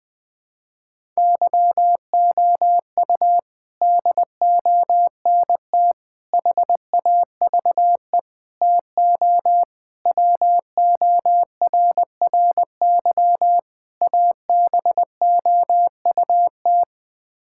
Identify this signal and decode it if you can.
Morse code, heard as a keyed tone, about 21 wpm, 700 Hz. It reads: YOU DONT HAVE TO WORRY ABOUT
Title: Sets of 6 Words - 20wpm